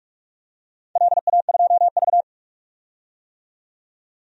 Morse code, heard as a keyed tone, about 45 wpm, 700 Hz. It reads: LA2V
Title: DXCC-100 Call Signs - 4 Characters - 45wpm